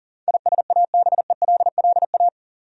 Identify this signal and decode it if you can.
Morse code, heard as a keyed tone, about 37 wpm, 700 Hz. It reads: ISABELLA
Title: USA Common Names - 40wpm